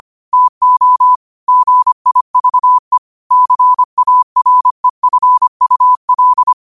Morse code, heard as a keyed tone, about 25 wpm, 1000 Hz. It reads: TO GIVE CAREFUL
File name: qotdfiles / 25wpm